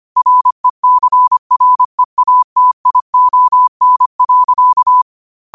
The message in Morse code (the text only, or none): RECREATION.